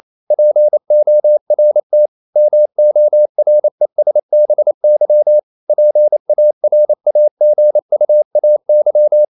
Morse code, heard as a keyed tone, about 28 wpm, 600 Hz. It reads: PORT MORESBY PARAGUAY
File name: CW / Practice / NationalCapitals28